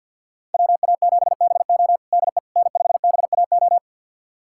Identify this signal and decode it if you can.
Morse code, heard as a keyed tone, about 49 wpm, 700 Hz. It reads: WA7BO DE N5BAO
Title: USA Callsign DE Callsign - 5 Characters - 50wpm